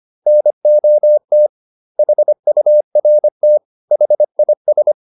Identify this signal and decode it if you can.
Morse code, heard as a keyed tone, about 25 wpm, 600 Hz. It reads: NOT HURT HIS
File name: the old man and the sea part 1 25wpm